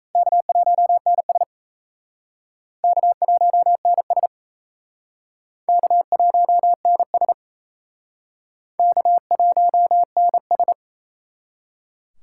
Morse code, heard as a keyed tone, about 38 wpm, 700 Hz. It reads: K1NS K1NS K1NS K1NS